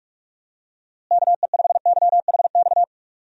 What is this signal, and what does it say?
Morse code, heard as a keyed tone, about 46 wpm, 700 Hz. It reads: KE5YHX
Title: USA Call Signs - 6 Characters - Repeated 3x - 45wpm